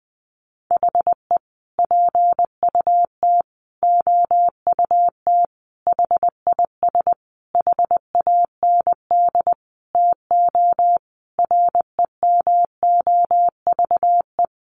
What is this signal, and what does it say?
Morse code, heard as a keyed tone, about 20 wpm, 700 Hz. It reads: HE PUT OUT HIS HAND TO REMOVE
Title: Sentences from Top 800 Words 20wpm